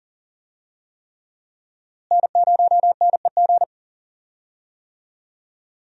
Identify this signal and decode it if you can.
Morse code, heard as a keyed tone, about 40 wpm, 700 Hz. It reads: N0NEG
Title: USA Call Signs - 5 Characters - Repeated 3x - 40wpm